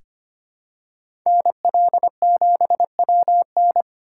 Morse code, heard as a keyed tone, about 25 wpm, 700 Hz. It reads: NL7WN
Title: USA Call Signs - 5 Characters - 3x Character Spacing - 25wpm